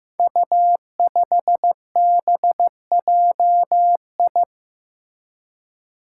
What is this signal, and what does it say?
Morse code, heard as a keyed tone, about 15 wpm, 700 Hz. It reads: U5BJI
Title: Five Challenging Letter-Number Combinations 15wpm